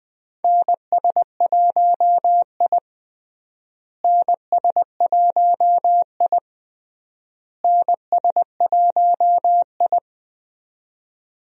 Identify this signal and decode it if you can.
Morse code, heard as a keyed tone, about 20 wpm, 700 Hz. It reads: NS1I NS1I NS1I
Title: USA Call Signs - 4 Characters - Repeated 3x - 20wpm